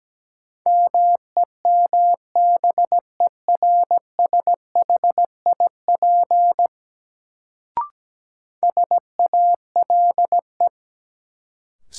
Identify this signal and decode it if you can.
Morse code, heard as a keyed tone, about 17 wpm, 700 Hz. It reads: MEMBERSHIP SALE
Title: Sentences from Top 1400 Words - New Word Drill - 17wpm